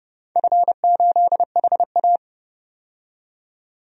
Morse code, heard as a keyed tone, about 29 wpm, 700 Hz. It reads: F8HA
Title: DXCC-100 Call Signs - 4 Characters - 30wpm